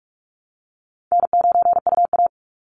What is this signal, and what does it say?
Morse code, heard as a keyed tone, about 44 wpm, 700 Hz. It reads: N9UA
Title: Easy Call Signs Repeated 2x - 45wpm